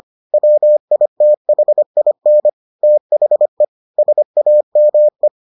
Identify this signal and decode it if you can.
Morse code, heard as a keyed tone, about 25 wpm, 600 Hz. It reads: WITHIN THE SAME